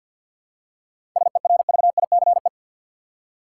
Morse code, heard as a keyed tone, about 53 wpm, 700 Hz. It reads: SERVICE
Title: Seven Letter Words - 50wpm